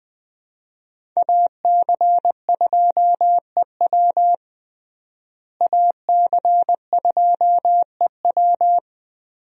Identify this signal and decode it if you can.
Morse code, heard as a keyed tone, about 21 wpm, 700 Hz. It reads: AC2EW AC2EW
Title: USA Call Signs - 5 Characters - Repeated 3x - 20wpm